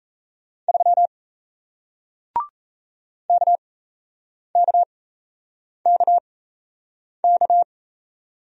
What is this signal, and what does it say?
Morse code, heard as a keyed tone, about 38 wpm, spaced as for 8 wpm, 700 Hz. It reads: 3 KKKK